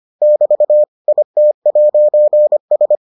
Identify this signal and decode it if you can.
Morse code, heard as a keyed tone, about 25 wpm, 600 Hz. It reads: <BT> IT'S